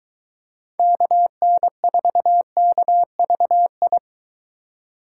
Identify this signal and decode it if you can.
Morse code, heard as a keyed tone, about 23 wpm, 700 Hz. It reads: KN4KVI